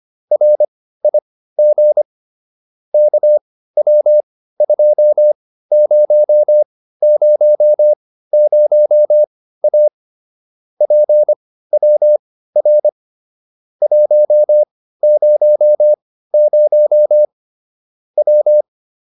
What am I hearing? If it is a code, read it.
Morse code, heard as a keyed tone, about 25 wpm, spaced as for 15 wpm, 600 Hz. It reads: RIG KW2000A PWR 100 W